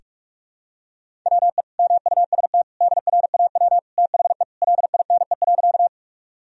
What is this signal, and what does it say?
Morse code, heard as a keyed tone, about 42 wpm, 700 Hz. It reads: WE MUST DRAW THE LINE.
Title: Sentences from Top 1000 Words - Review Part 1 - 45wpm